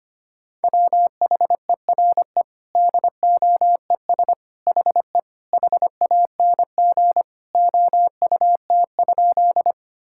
Morse code, heard as a keyed tone, about 25 wpm, 700 Hz. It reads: WHERE DOES HE HANG OUT?
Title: Sentences from Top 1600 Words 25wpm